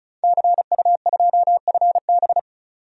Morse code, heard as a keyed tone, about 34 wpm, 700 Hz. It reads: CU2FB
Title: DXCC-100 Call Signs - 5 Characters - 35wpm